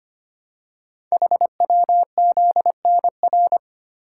Morse code, heard as a keyed tone, about 25 wpm, 700 Hz. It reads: HWZNR
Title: Code Groups - 5 Letter:Numbers - 25wpm